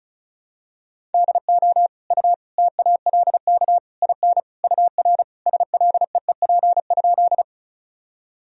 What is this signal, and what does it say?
Morse code, heard as a keyed tone, about 36 wpm, 700 Hz. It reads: DO U TALK IN UR SLEEP?
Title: Sentences from Top 500 Words and Abbreviations 35wpm